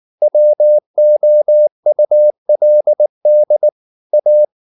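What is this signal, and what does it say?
Morse code, heard as a keyed tone, about 19 wpm, 600 Hz. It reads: WOULD A
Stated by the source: Morse code at 19 wpm from CW med Gnister, reading Ken Follett's novel Never in English.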